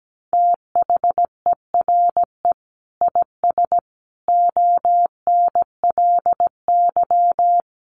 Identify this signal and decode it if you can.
Morse code, heard as a keyed tone, about 17 wpm, 700 Hz. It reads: THERE IS ONLY
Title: Sentences from Top 300 Words 17wpm